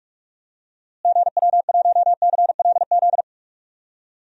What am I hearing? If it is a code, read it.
Morse code, heard as a keyed tone, about 41 wpm, 700 Hz. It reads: GW1CLZ
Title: DXCC-100 Call Signs - 6 Characters - Repeated 3x - 45wpm